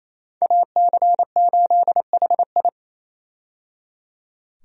Morse code, heard as a keyed tone, about 27 wpm, 700 Hz. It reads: AC8HI
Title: USA Call Signs - 5 Characters - 28wpm